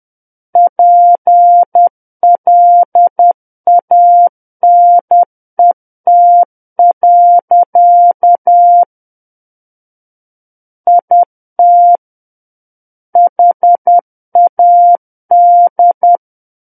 Morse code, heard as a keyed tone, about 10 wpm, 700 Hz. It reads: PLANET. IT HAD
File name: Book One / 01-Chapter 1 - 10 WPM